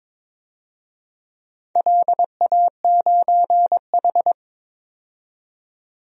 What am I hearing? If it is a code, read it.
Morse code, heard as a keyed tone, about 22 wpm, 700 Hz. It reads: LA9H
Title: DXCC-100 Call Signs - 4 Characters - 22wpm